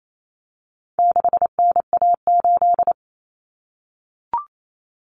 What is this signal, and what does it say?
Morse code, heard as a keyed tone, about 28 wpm, 700 Hz. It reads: 6NA8
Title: Common Cut Numbers - 4-Digit 28wpm